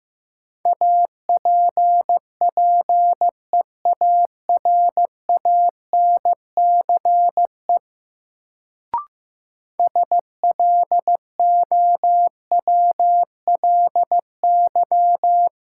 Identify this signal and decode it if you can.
Morse code, heard as a keyed tone, about 16 wpm, 700 Hz. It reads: APPEARANCE SLOWLY
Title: Sentences from Top 1100 Words - New Word Drill Alt - 15wpm